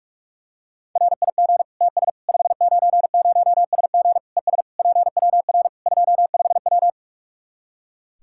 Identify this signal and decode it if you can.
Morse code, heard as a keyed tone, about 46 wpm, 700 Hz. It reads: RIG TS 590SG ES PWR 25W
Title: Practice QSO - Exchange 2 - Info - 45wpm